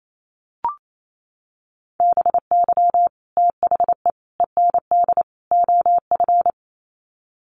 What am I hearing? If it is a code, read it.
Morse code, heard as a keyed tone, about 28 wpm, 700 Hz. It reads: BY THE END OF
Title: Sets of 4 Words 28wpm